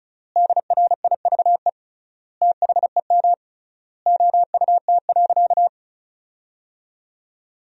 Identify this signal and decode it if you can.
Morse code, heard as a keyed tone, about 34 wpm, 700 Hz. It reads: DRIVE THEM OUT.
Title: 3x Word-Spacing - Sentences from Top 500 Words - 35wpm